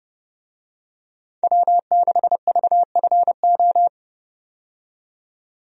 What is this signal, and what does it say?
Morse code, heard as a keyed tone, about 30 wpm, 700 Hz. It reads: W6VFO
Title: USA Call Signs - 5 Characters - 30wpm